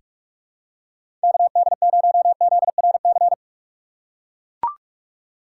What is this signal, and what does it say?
Morse code, heard as a keyed tone, about 41 wpm, 700 Hz. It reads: KD0ZRC
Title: USA Call Signs - 3x Character Spacing - 45wpm